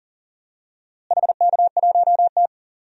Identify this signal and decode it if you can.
Morse code, heard as a keyed tone, about 37 wpm, 700 Hz. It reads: HK1T
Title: DXCC-100 Call Signs - 4 Characters - Repeated 3x - 40wpm